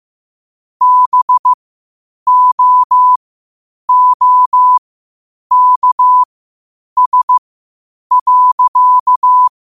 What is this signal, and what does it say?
Morse code, heard as a keyed tone, about 15 wpm, spaced as for 8 wpm, 1000 Hz. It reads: BOOKS.
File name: qotdfiles / 8wpm